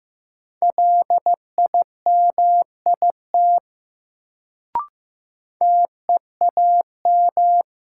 Morse code, heard as a keyed tone, about 15 wpm, 700 Hz. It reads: LIMIT TEAM
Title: Sentences from Top 1500 Words - New Word Drill Alt - 15wpm